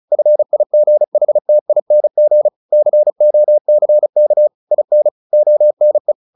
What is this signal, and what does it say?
Morse code, heard as a keyed tone, about 34 wpm, 600 Hz. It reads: FIGHTING COCK IN ONE
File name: the old man and the sea / the old man and the sea 1 35wpm